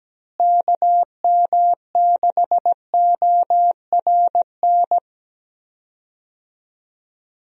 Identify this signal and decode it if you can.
Morse code, heard as a keyed tone, about 17 wpm, 700 Hz. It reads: KM6ORN
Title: USA Call Signs - 6 Characters - Repeated 2x - 17wpm